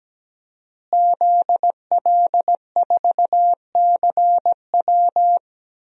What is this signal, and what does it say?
Morse code, heard as a keyed tone, about 17 wpm, 700 Hz. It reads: ZL4CW
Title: DXCC-100 Call Signs - 5 Characters - Repeated 2x - 17wpm